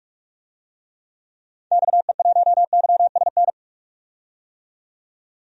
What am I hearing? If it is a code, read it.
Morse code, heard as a keyed tone, about 42 wpm, 700 Hz. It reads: XE1YSN
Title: DXCC-100 Call Signs - 6 Characters - 45wpm